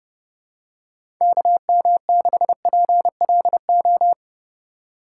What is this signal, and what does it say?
Morse code, heard as a keyed tone, about 30 wpm, 700 Hz. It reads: KM6PLO
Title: USA Call Signs - 6 Characters - 2x Character Spacing - 30wpm